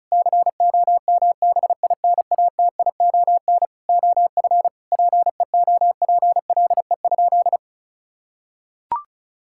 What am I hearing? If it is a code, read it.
Morse code, heard as a keyed tone, about 36 wpm, 700 Hz. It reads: COMBINATION OF PEOPLE?
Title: Sentences from Top 1000 Words - Review Part 6 - 35wpm